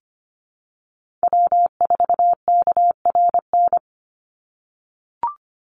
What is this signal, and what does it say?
Morse code, heard as a keyed tone, about 25 wpm, 700 Hz. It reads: W4KRN
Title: NA SOTA Chaser Callsigns 25wpm